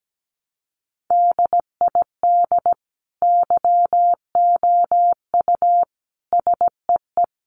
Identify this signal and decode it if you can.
Morse code, heard as a keyed tone, about 17 wpm, 700 Hz. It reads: DID YOU SEE
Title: Sentences from Top 100 Words 17wpm